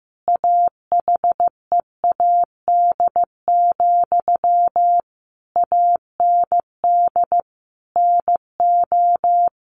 Morse code, heard as a keyed tone, about 15 wpm, 700 Hz. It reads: AHEAD, AND NO
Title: Sentences from Top 1100 Words 15wpm